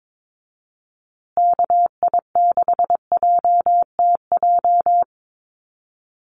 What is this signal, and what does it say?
Morse code, heard as a keyed tone, about 22 wpm, 700 Hz. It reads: KI6JTJ
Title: USA Call Signs Repeated 2x 22wpm